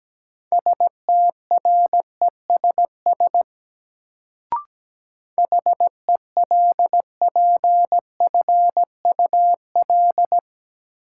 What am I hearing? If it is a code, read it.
Morse code, heard as a keyed tone, about 17 wpm, 700 Hz. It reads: STRESS HELPFUL
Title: Sentences from Top 1000 Words - Review Part 6 - New Word Drill - 17wpm